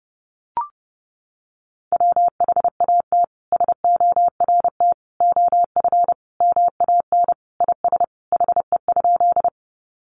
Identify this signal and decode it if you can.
Morse code, heard as a keyed tone, about 30 wpm, 700 Hz. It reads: WHAT SORT OF MAN IS HE?
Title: Sentences from Top 1300 Words 30wpm